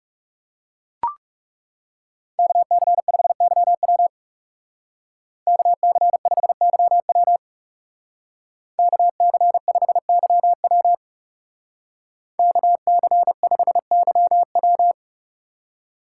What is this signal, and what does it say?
Morse code, heard as a keyed tone, about 36 wpm, 700 Hz. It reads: KC5YW KC5YW KC5YW KC5YW
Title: Speed Racing - USA Call Signs 30wpm